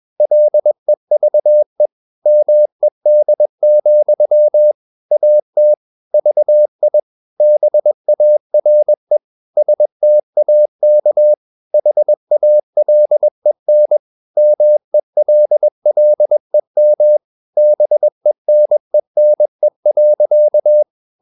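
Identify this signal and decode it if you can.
Morse code, heard as a keyed tone, about 21 wpm, 600 Hz. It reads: LEVE MED, AT VI BARE STAK HALEN MELLEM BENENE.
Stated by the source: Morse practice at 21 wpm, slow-speed holiday exercise.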